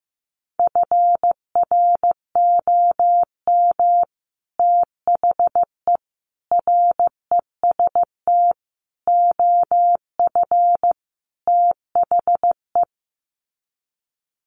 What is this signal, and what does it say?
Morse code, heard as a keyed tone, about 15 wpm, 700 Hz. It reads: FROM THE REST OF THE
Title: Sets of 5 Words 15wpm